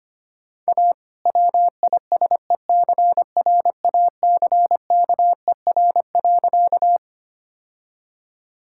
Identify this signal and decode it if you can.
Morse code, heard as a keyed tone, about 25 wpm, 700 Hz. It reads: A WISECRACKER.
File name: practice / 25wpm / All-Age Appropriate Jokes - 25wpm